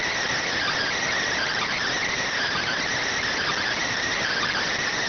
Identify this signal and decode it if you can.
Emergency Alert System (SAME) tones, heard as harsh buzzing bursts. It is heard from 0.5 to 4.6 s.